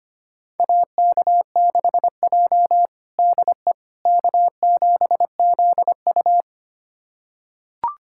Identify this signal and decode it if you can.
Morse code, heard as a keyed tone, about 25 wpm, 700 Hz. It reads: AK6J DE K7ZU
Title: USA Callsign DE Callsign - 4 Characters - 25wpm